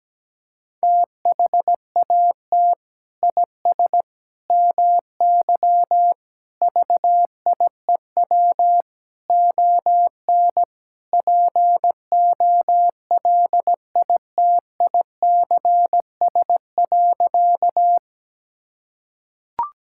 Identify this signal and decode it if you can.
Morse code, heard as a keyed tone, about 17 wpm, 700 Hz. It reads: THAT IS MY VIEW ON POLITICS.
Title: Sentences from Top 1000 Words - Review Part 6 - 17wpm